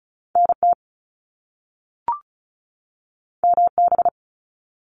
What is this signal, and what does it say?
Morse code, heard as a keyed tone, about 36 wpm, 700 Hz. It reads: NT MB
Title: Canadian Provinces and Territories 35wpm